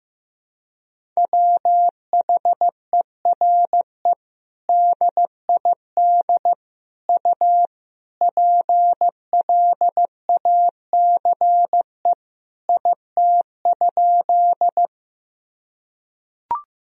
Morse code, heard as a keyed tone, about 15 wpm, 700 Hz. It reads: WHERE DID U PLACE IT?
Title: Sentences from Top 200 Words and Abbreviations 15wpm